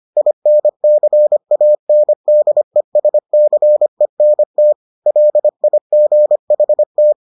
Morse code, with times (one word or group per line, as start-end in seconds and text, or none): 0.2-4.7 INCANDESCENT
5.1-7.1 LIGHT